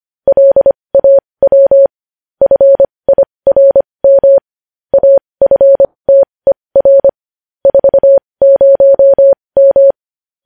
Morse code, heard as a keyed tone, about 25 wpm, spaced as for 20 wpm, 550 Hz. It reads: LAW FIRM AFTER 40M
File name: podcast / News headlines-1x-25x20